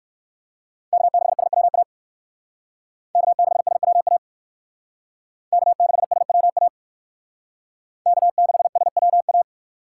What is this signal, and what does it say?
Morse code, heard as a keyed tone, about 56 wpm, 700 Hz. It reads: K6SWA K6SWA K6SWA K6SWA